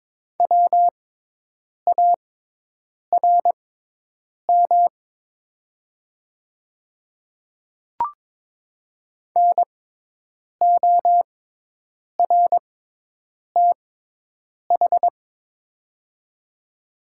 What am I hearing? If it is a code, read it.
Morse code, heard as a keyed tone, about 22 wpm, spaced as for 8 wpm, 700 Hz. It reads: WARM NORTH
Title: 6x Character Spacing - Top 500 Words - 22wpm